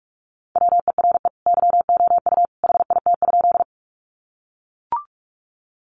Morse code, heard as a keyed tone, about 44 wpm, 700 Hz. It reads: WERE YOU HIT?
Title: Sentences from Top 1200 Words 45wpm